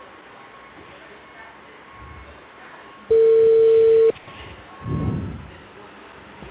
Ringback tone (China, 450 Hz).